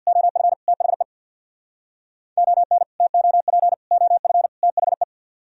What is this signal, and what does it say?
Morse code, heard as a keyed tone, about 65 wpm, 700 Hz. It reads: OF THE ON TOP OF THE